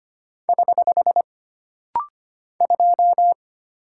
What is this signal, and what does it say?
Morse code, heard as a keyed tone, about 25 wpm, 700 Hz. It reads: <HH> 2
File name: Single Character (Letter:Number:Punctuation:Prosign) - Rapid-Fire - 25wpm